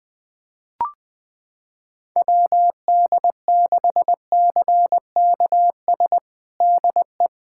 Morse code, heard as a keyed tone, about 20 wpm, 700 Hz. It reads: WD6CKS DE
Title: USA Callsign DE Callsign - 6 Characters - 20wpm